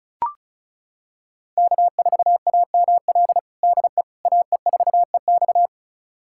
Morse code, heard as a keyed tone, about 36 wpm, 700 Hz. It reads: K4AML DE AE4EX
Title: USA Callsign DE Callsign - 5 Characters - 35wpm